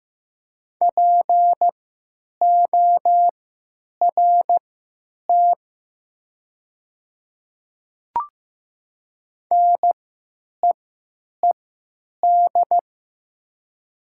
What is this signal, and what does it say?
Morse code, heard as a keyed tone, about 15 wpm, spaced as for 9 wpm, 700 Hz. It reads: PORT NEED
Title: 3x Character Spacing - Top 200 Words - 15wpm